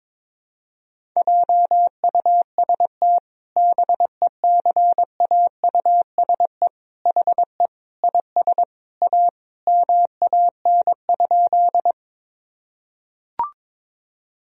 Morse code, heard as a keyed tone, about 22 wpm, 700 Hz. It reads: JUST BECAUSE HE IS A MAN?
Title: Sentences from Top 1000 Words - Review Part 3 - 22wpm